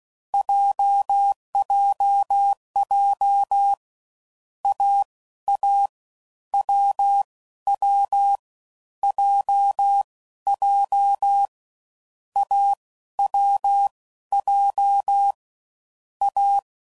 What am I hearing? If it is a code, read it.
Morse code, heard as a keyed tone, about 16 wpm, spaced as for 12 wpm, 800 Hz. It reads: JJJ AA WW JJ AWJ A
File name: Combinations / Difficult Combos / AWJ